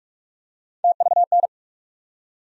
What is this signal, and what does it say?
Morse code, heard as a keyed tone, about 44 wpm, 700 Hz. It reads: TVN
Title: USA Call Sign Postfixes - 3 Letters 45wpm